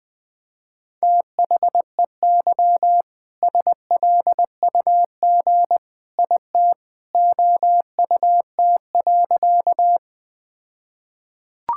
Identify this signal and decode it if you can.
Morse code, heard as a keyed tone, about 20 wpm, 700 Hz. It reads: THEY SLUG IT OUT.
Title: All-Age Appropriate Jokes - 20wpm